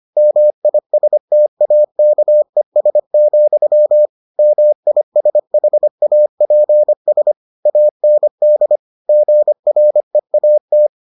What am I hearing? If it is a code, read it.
Morse code, heard as a keyed tone, about 25 wpm, 600 Hz. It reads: MISTAKES, MISHAPS AND GREAT